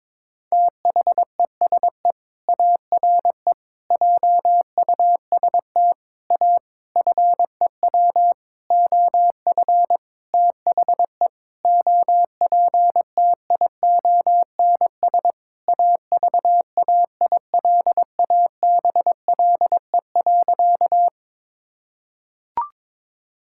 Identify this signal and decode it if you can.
Morse code, heard as a keyed tone, about 22 wpm, 700 Hz. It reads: THESE ARE JUST A FEW OF THE OPTIONS AVAILABLE.
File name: Sentences with 7-Word Phrases - 22wpm